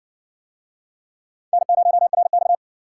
Morse code, heard as a keyed tone, about 60 wpm, 700 Hz. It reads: N0RX